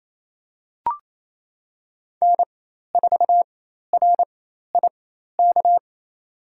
Morse code, heard as a keyed tone, about 28 wpm, spaced as for 13 wpm, 700 Hz. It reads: N4RIK
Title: USA Call Signs - 5 Characters - 4x Character Spacing - 28wpm